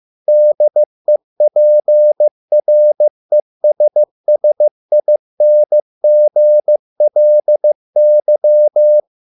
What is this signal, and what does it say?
Morse code, heard as a keyed tone, about 15 wpm, 600 Hz. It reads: DEPRESSINGLY